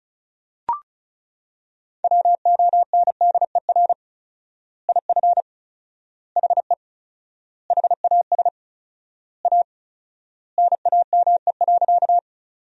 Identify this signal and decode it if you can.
Morse code, heard as a keyed tone, about 36 wpm, 700 Hz. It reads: WONDER IF HE HAS A NAME.
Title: 4x Word-Spacing - Sentences from Top 500 Words - 35wpm